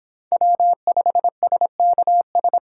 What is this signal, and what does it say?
Morse code, heard as a keyed tone, about 25 wpm, 700 Hz. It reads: W5SKS